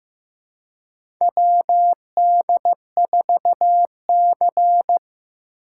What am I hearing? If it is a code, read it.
Morse code, heard as a keyed tone, about 15 wpm, 700 Hz. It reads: WD4C
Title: USA Call Signs - 4 Characters - 4x Character Spacing - 15wpm